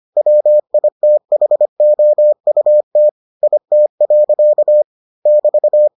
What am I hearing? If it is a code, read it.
Morse code, heard as a keyed tone, about 25 wpm, 600 Hz. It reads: WITHOUT IT. <BT>